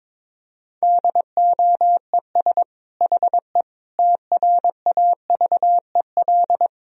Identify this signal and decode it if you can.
Morse code, heard as a keyed tone, about 22 wpm, 700 Hz. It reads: DOES HE TRAVEL
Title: Sentences from Top 1000 Words - Review Part 2 - 22wpm